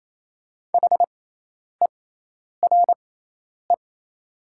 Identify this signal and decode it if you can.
Morse code, heard as a keyed tone, about 28 wpm, spaced as for 10 wpm, 700 Hz. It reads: HERE
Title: 6x Character Spacing - Top 200 Words - 28wpm